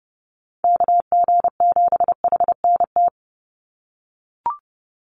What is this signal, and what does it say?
Morse code, heard as a keyed tone, about 30 wpm, 700 Hz. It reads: KG7HNT
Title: USA Call Signs 30wpm